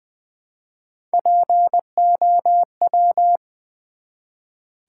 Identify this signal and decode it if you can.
Morse code, heard as a keyed tone, about 20 wpm, 700 Hz. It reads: POW
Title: Build A Word - 1 Letter At A Time - 5-Letter Words - 20wpm